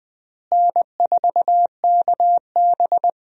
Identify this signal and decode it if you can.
Morse code, heard as a keyed tone, about 20 wpm, 700 Hz. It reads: N4KB
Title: USA Call Signs - 4 Characters - Repeated 2x - 20wpm